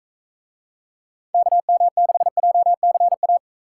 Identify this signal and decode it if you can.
Morse code, heard as a keyed tone, about 41 wpm, 700 Hz. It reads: KM6JCA